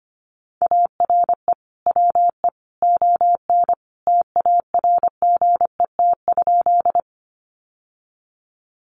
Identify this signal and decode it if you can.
Morse code, heard as a keyed tone, about 25 wpm, 700 Hz. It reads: ARE WE ON TARGET?
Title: Sentences from Top 1200 Words 25wpm